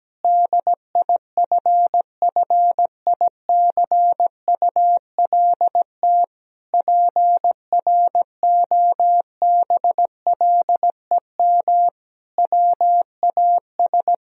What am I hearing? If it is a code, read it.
Morse code, heard as a keyed tone, about 17 wpm, 700 Hz. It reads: DIFFICULT PROBLEM WAS
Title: Sentences from Top 1000 Words - Review Part 2 - 17wpm